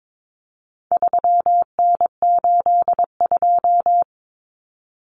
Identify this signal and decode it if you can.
Morse code, heard as a keyed tone, about 22 wpm, 700 Hz. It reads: 3N82
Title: Common Cut Numbers - 4-Digit 22wpm